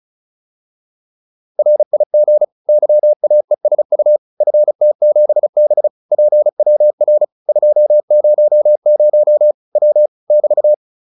Morse code, heard as a keyed tone, about 34 wpm, 600 Hz. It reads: RIG YAESU FT7B PWR 200 W <BT>
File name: QSO407 35